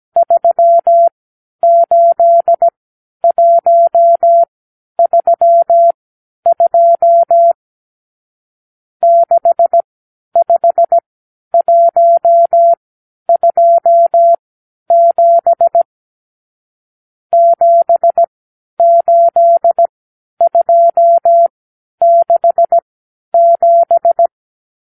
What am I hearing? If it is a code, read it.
Morse code, heard as a keyed tone, about 17 wpm, spaced as for 10 wpm, 680 Hz. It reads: 38132 65127 78267
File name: k7qo / Lesson 54 (9)